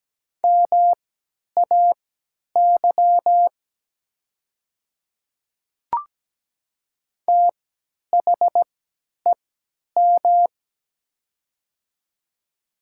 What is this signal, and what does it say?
Morse code, heard as a keyed tone, about 17 wpm, spaced as for 10 wpm, 700 Hz. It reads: MAY THEM
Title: 3x Character Spacing - Top 100 Words - 17wpm